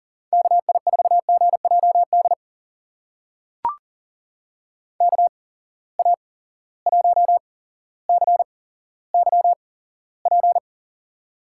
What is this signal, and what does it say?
Morse code, heard as a keyed tone, about 40 wpm, spaced as for 17 wpm, 700 Hz. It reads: KI4GJD KA1CYP
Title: USA Call Signs - 6 Characters - 8x Character Spacing - 40wpm